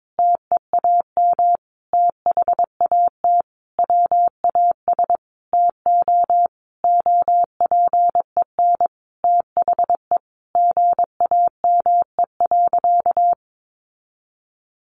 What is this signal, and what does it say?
Morse code, heard as a keyed tone, about 22 wpm, 700 Hz. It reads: TEAM THAT WAS TO OPEN THE GAME.
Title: Sentences from Top 1500 Words 22wpm